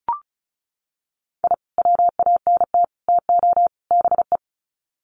Morse code, heard as a keyed tone, about 34 wpm, 700 Hz. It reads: I WANT TO BE
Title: Sets of 4 Words 35wpm